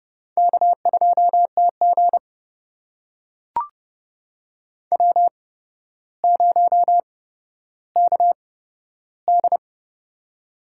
Morse code, heard as a keyed tone, about 31 wpm, spaced as for 13 wpm, 700 Hz. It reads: K2TG W0KD
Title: USA Call Signs - 8x Character Spacing - 30wpm